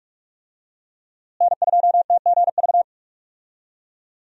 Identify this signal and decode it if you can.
Morse code, heard as a keyed tone, about 44 wpm, 700 Hz. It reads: N2TGV
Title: USA Call Signs - 5 Characters - Repeated 3x - 45wpm